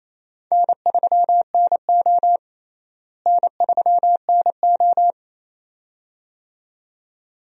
Morse code, heard as a keyed tone, about 27 wpm, 700 Hz. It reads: N3NO N3NO
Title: USA Call Signs - 4 Characters - Repeated 2x - 28wpm